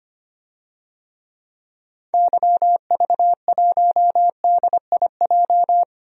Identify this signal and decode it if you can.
Morse code, heard as a keyed tone, about 25 wpm, 700 Hz. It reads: YV1DIJ
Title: DXCC-100 Call Signs - 6 Characters - 25wpm